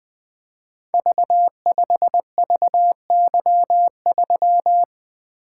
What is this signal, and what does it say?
Morse code, heard as a keyed tone, about 20 wpm, 700 Hz. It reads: V5VY3
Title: Code Groups - 5 Letter:Numbers - 20wpm